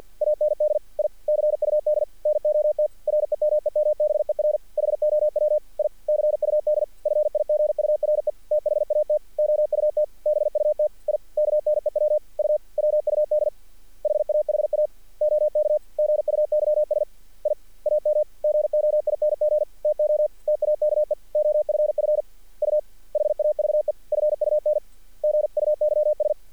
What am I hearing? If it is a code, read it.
Morse code, heard as a keyed tone, about 46 wpm, 610 Hz. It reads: AND I CUD NOT REMEMBER HOW I CUD FIGURE THAT OUT BUT I KNEW U WUD HAHA OK GUYS I AM GOING TO TAKE OFF U HAVE FUN GUYS
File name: archived / CRSnet